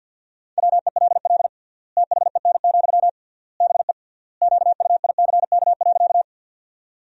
Morse code, heard as a keyed tone, about 53 wpm, 700 Hz. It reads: WELL THEN, BE QUICK.
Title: 3x Word-Spacing - Sentences from Top 500 Words - 50wpm